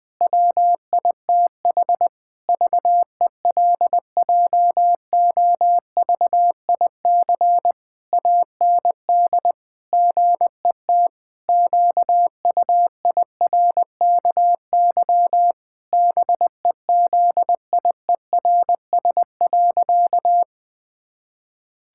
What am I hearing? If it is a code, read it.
Morse code, heard as a keyed tone, about 20 wpm, 700 Hz. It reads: WITH VELJOVIC AND GET QUIRKY BEZIERS.
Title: Pangram Challenge v2 - 20wpm